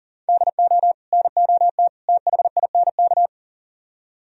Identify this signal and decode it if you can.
Morse code, heard as a keyed tone, about 52 wpm, 700 Hz. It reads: DO NOT THINK